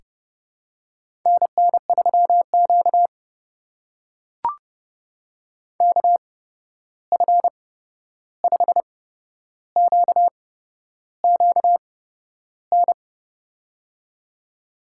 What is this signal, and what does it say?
Morse code, heard as a keyed tone, about 30 wpm, spaced as for 11 wpm, 700 Hz. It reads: NN3Q KF5QQN
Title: USA Call Signs - 8x Character Spacing - 30wpm